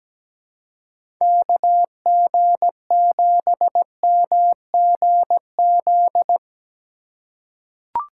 Morse code, heard as a keyed tone, about 17 wpm, 700 Hz. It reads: KG7MGZ
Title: USA Call Signs - 2x Character Spacing - 17wpm